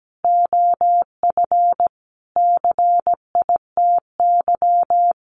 Morse code, heard as a keyed tone, about 17 wpm, 700 Hz. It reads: OF CITY
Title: Sentences from Top 1300 Words 17wpm